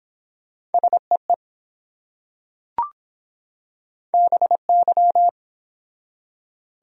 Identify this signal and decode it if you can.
Morse code, heard as a keyed tone, about 23 wpm, 700 Hz. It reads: SEE BY